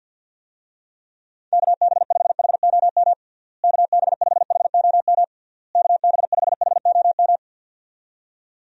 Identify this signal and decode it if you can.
Morse code, heard as a keyed tone, about 53 wpm, 700 Hz. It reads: KB5HOM KB5HOM KB5HOM
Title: USA Call Signs - 6 Characters - Repeated 3x - 50wpm